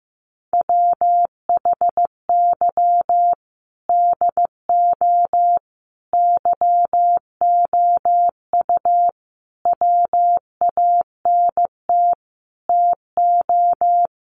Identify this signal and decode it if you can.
Morse code, heard as a keyed tone, about 15 wpm, 700 Hz. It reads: WHY DO YOU WANT TO
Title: Sentences from Top 200 Words 15wpm